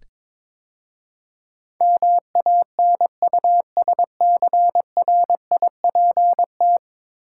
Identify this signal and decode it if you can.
Morse code, heard as a keyed tone, about 22 wpm, 700 Hz. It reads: MANUSCRIPT
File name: practice / 22wpm / 10 Letter Words - First 3 Letters Given - 22wpm